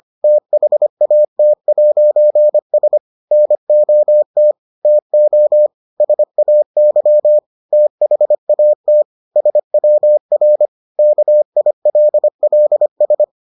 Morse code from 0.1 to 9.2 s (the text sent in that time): THAT'S NOT TO SAY THAT